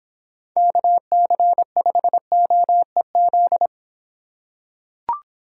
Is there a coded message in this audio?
Yes: Morse code from 0.6 to 3.7 s, heard as a keyed tone, about 26 wpm, 700 Hz. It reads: KC5OEZ